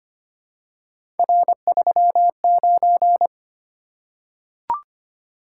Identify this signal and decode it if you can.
Morse code, heard as a keyed tone, about 25 wpm, 700 Hz. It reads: R39
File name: Three Letter-Number Combinations - 25wpm